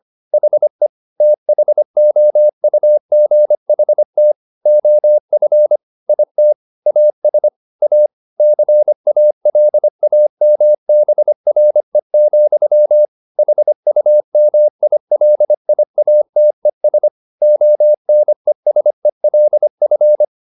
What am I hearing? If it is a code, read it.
Morse code, heard as a keyed tone, about 25 wpm, 600 Hz. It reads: HE THOUGHT OF IT AS A CALAMBRE, HUMILIATES ONESELF
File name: the old man and the sea part 1 25wpm